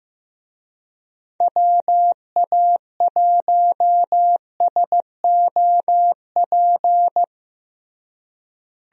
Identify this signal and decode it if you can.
Morse code, heard as a keyed tone, about 15 wpm, 700 Hz. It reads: WA1SOP
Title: USA Call Signs - 6 Characters - Repeated 2x - 15wpm